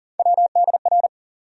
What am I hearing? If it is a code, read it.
Morse code, heard as a keyed tone, about 40 wpm, 700 Hz. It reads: WDR
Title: USA Call Sign Postfixes - 3 Letters 40wpm